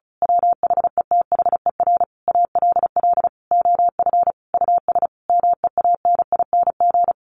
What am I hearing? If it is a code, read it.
Morse code, heard as a keyed tone, about 34 wpm, 700 Hz. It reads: WHETHER ALL OF US MEANING
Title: A Sound Of Thunder 35wpm